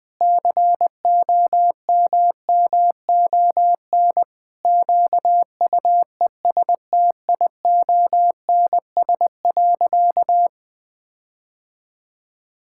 Morse code, heard as a keyed tone, about 20 wpm, 700 Hz. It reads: COMMON QUESTIONS.
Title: Sentences with 4-Word Phrases - 20wpm